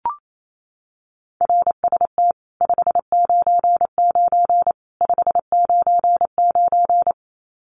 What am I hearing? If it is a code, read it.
Morse code, heard as a keyed tone, about 28 wpm, 700 Hz. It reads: RST 599 599
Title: Common QSO Phrases 28wpm